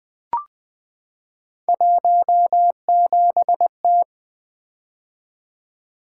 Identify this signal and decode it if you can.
Morse code, heard as a keyed tone, about 20 wpm, 700 Hz. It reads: 17T
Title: Three Letter-Number Combinations - 20wpm